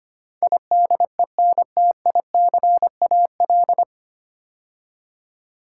Morse code, heard as a keyed tone, about 25 wpm, 700 Hz. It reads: IDENTICAL
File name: practice / 25wpm / 9 Letter Words - First 3 Letters Given - 25wpm